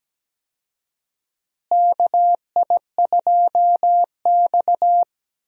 Morse code, heard as a keyed tone, about 17 wpm, 700 Hz. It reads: KI2X
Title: USA Call Signs - 4 Characters - Repeated 3x - 17wpm